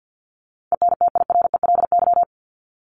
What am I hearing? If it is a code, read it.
Morse code, heard as a keyed tone, about 48 wpm, 700 Hz. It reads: ENTIRELY
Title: LY Suffix - Top 100 Words with Base Words 50wpm